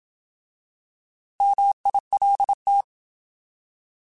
Morse code, heard as a keyed tone, about 27 wpm, 800 Hz. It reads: MILT